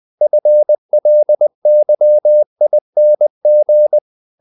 Morse code, heard as a keyed tone, about 20 wpm, 600 Hz. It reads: FLYING
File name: the old man and the sea / the old man and the sea part 1 20wpm